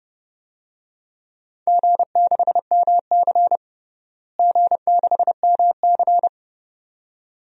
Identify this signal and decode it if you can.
Morse code, heard as a keyed tone, about 30 wpm, 700 Hz. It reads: G6MC G6MC
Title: DXCC-100 Call Signs - 4 Characters - Repeated 2x - 30wpm